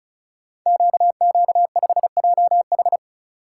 Morse code, heard as a keyed tone, about 36 wpm, 690 Hz. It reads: QQ5JH
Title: Code Groups - 5 Letter:Numbers - 35wpm